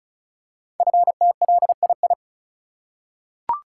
Morse code, heard as a keyed tone, about 36 wpm, 700 Hz. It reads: FTLII
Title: Code Groups - 5 Letters - 35wpm